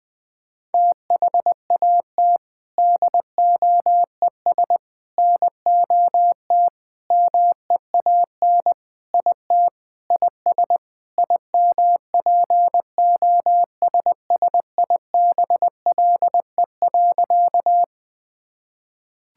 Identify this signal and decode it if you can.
Morse code, heard as a keyed tone, about 20 wpm, 700 Hz. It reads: THAT DOES NOT MEAN IT IS IMPOSSIBLE.
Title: Sentences with 4-Word Phrases - 20wpm